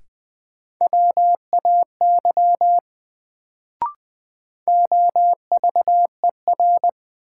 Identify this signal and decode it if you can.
Morse code, heard as a keyed tone, about 20 wpm, 700 Hz. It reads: WAY OVER
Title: Sets of 2 Words - New Word Drill - 20wpm